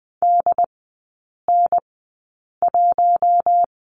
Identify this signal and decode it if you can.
Morse code, heard as a keyed tone, about 20 wpm, spaced as for 8 wpm, 700 Hz. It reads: DN1
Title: 081 Lesson 14e - Number 2 - Three Characters 20wpm